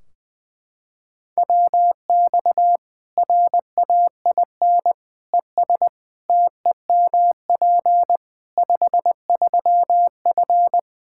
Morse code, heard as a keyed tone, about 20 wpm, 700 Hz. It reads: WX RAIN ES TEMP 53F
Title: Practice QSO - Exchange 2 - Info - 20wpm